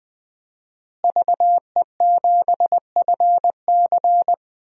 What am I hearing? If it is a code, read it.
Morse code, heard as a keyed tone, about 21 wpm, 700 Hz. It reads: VE7FC
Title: DXCC-100 Call Signs - 5 Characters - Repeated 2x - 20wpm